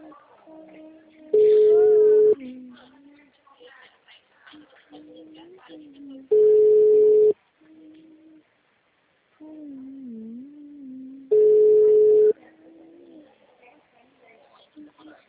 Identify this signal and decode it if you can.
Ringback tone (Europe, 425 Hz), with speech.
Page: Limeira - SP